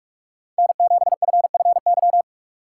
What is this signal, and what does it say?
Morse code, heard as a keyed tone, about 46 wpm, 700 Hz. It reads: N7FFY
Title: USA Call Signs - 5 Characters - 2x Character Spacing - 45wpm